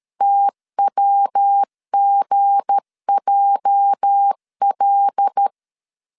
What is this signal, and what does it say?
Morse code, heard as a keyed tone, about 13 wpm, 800 Hz. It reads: TWGJL
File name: CW / DET / 1 9 1500C S A 12wpm 3800